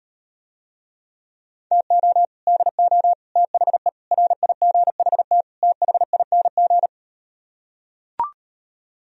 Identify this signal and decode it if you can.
Morse code, heard as a keyed tone, about 38 wpm, 700 Hz. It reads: TO DO THE RIGHT THING